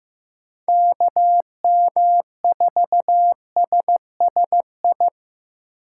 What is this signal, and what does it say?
Morse code, heard as a keyed tone, about 15 wpm, 700 Hz. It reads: KM4SSI
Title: USA Call Signs - 6 Characters - 8x Character Spacing - 15wpm